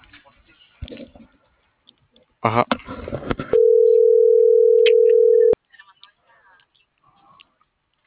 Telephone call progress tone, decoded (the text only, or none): ringback tone